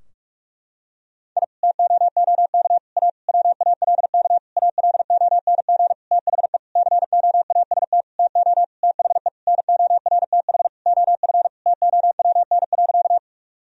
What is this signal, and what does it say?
Morse code, heard as a keyed tone, about 46 wpm, 700 Hz. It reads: I TOOK A WALK ALONG THE COAST TO THE NORTH OF TOWN.
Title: Sentences from Top 1000 Words - Review Part 4 - 45wpm